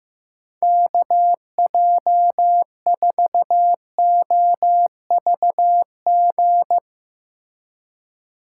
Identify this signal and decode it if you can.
Morse code, heard as a keyed tone, about 15 wpm, 700 Hz. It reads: KJ4OVG
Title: USA Call Signs - 6 Characters - 3x Character Spacing - 15wpm